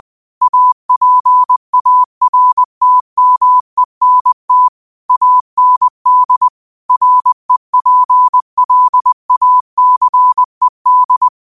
Morse code, heard as a keyed tone, about 20 wpm, 1000 Hz. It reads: APARTMENT AND REPLACED
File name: 2025-06-25-20wpm